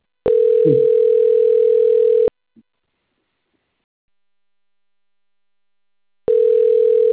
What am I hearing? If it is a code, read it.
Ringback tone (North America), 440 + 480 Hz.
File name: rg-1600-unknown- - - .73193